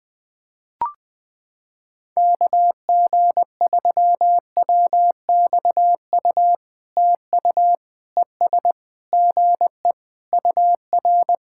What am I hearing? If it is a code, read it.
Morse code, heard as a keyed tone, about 21 wpm, 700 Hz. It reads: KG3WXU TU ES GE UR
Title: POTA QSO Practice - 20wpm